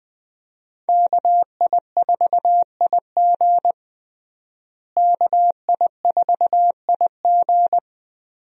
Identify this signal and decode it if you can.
Morse code, heard as a keyed tone, about 21 wpm, 700 Hz. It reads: KI4IG KI4IG
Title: USA Call Signs - 5 Characters - Repeated 3x - 20wpm